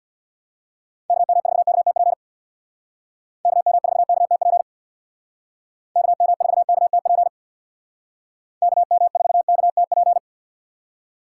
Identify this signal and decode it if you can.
Morse code, heard as a keyed tone, about 61 wpm, 700 Hz. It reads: KM4KTP KM4KTP KM4KTP KM4KTP
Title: Speed Racing - USA 2x3 Call Signs 50wpm